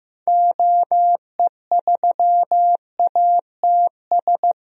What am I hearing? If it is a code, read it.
Morse code, heard as a keyed tone, about 15 wpm, 700 Hz. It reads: OE3ATS
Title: DXCC-100 Call Signs - 6 Characters - Repeated 3x - 15wpm